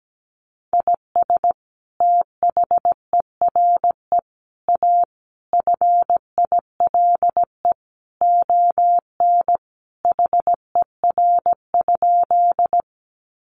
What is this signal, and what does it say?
Morse code, heard as a keyed tone, about 17 wpm, 700 Hz. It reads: IS THERE A FILE ON HER?
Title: Sentences from Top 1200 Words 17wpm